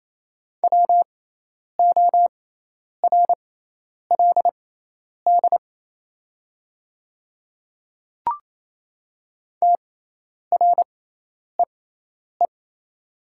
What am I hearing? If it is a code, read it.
Morse code, heard as a keyed tone, about 27 wpm, spaced as for 10 wpm, 690 Hz. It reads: WORLD TREE
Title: 6x Character Spacing - Top 300 Words - 28wpm